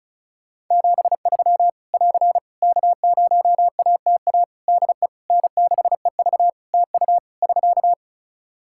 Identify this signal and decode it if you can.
Morse code, heard as a keyed tone, about 35 wpm, 700 Hz. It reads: 73 <AR> K0ATA DE N6EV TU <SK>
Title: Practice QSO - Exchange 4 - Ending - 35wpm